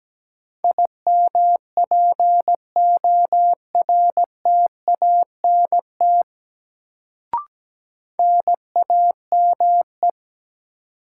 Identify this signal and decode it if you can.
Morse code, heard as a keyed tone, about 17 wpm, 700 Hz. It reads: IMPORTANT NAME
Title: Sets of 3 Words - New Word Drill - 17wpm